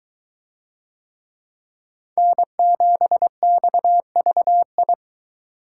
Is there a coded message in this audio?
Yes: Morse code from 2.2 to 4.9 s, heard as a keyed tone, about 20 wpm, 700 Hz. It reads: N7XVI